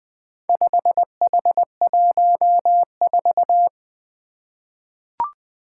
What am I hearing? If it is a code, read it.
Morse code, heard as a keyed tone, about 20 wpm, 700 Hz. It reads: 5H14
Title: Four Letter-Number Combinations - 20wpm